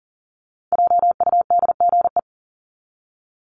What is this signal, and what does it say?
Morse code, heard as a keyed tone, about 40 wpm, 700 Hz. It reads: JUDGE
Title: Top 1751-2000 Words 40wpm